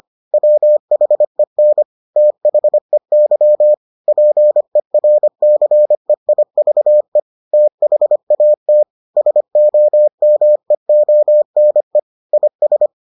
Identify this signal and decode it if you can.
Morse code, heard as a keyed tone, about 25 wpm, 600 Hz. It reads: WHEN THEY PERCEIVE THAT SOMEONE IS